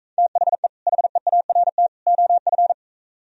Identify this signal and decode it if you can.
Morse code, heard as a keyed tone, about 45 wpm, 700 Hz. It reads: THE HEART OF